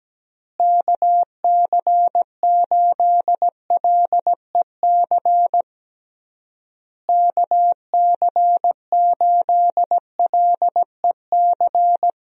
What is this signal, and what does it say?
Morse code, heard as a keyed tone, about 17 wpm, 700 Hz. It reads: KC8LEC KC8LEC
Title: USA Call Signs - 6 Characters - Repeated 3x - 17wpm